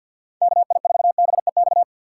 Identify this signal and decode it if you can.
Morse code, heard as a keyed tone, about 53 wpm, 700 Hz. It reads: KI4BEX
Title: USA Call Signs - 6 Characters - 3x Character Spacing - 50wpm